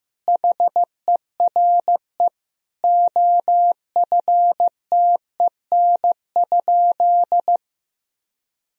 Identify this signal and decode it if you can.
Morse code, heard as a keyed tone, about 15 wpm, 700 Hz. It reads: HERE OFTEN?
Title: Sentences from Top 1000 Words - Review Part 1 - 15wpm